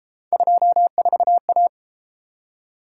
Morse code, heard as a keyed tone, about 31 wpm, 700 Hz. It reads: 24A